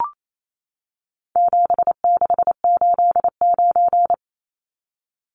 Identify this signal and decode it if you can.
Morse code, heard as a keyed tone, about 28 wpm, 700 Hz. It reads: 7689
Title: Numbers - 4-Digit 28wpm